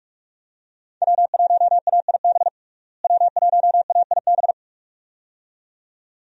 Morse code, heard as a keyed tone, about 44 wpm, 700 Hz. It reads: W1AIB W1AIB
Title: USA Call Signs - 5 Characters - Repeated 2x - 45wpm